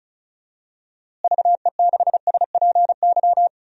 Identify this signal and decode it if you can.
Morse code, heard as a keyed tone, about 34 wpm, 700 Hz. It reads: VE6SPY
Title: DXCC-100 Call Signs - 6 Characters - Repeated 2x - 35wpm